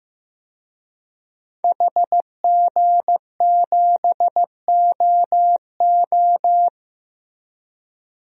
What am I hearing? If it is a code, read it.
Morse code, heard as a keyed tone, about 15 wpm, 700 Hz. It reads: HG7OO
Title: DXCC-100 Call Signs - 5 Characters - Repeated 3x - 15wpm